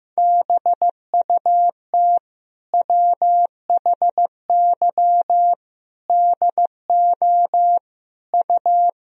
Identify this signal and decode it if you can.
Morse code, heard as a keyed tone, about 15 wpm, 700 Hz. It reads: BUT WHY DO U
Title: Sentences from Top 200 Words and Abbreviations 15wpm